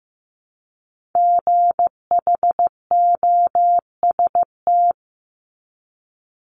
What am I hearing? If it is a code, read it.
Morse code, heard as a keyed tone, about 15 wpm, 700 Hz. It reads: GHOST
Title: Multi-letter Phonemes Repeated 3x 15wpm